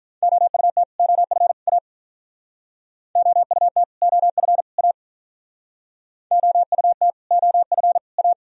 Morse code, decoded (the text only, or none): OUT OF A OUT OF A OUT OF A